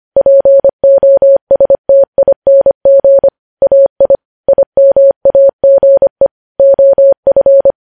Morse code, heard as a keyed tone, about 25 wpm, 550 Hz. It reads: POSTING AI IMAGE OF
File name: podcast / News headlines-1x-25x25